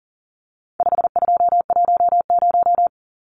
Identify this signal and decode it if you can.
Morse code, heard as a keyed tone, about 40 wpm, 700 Hz. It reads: 5210
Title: Numbers - 4-Digit 40wpm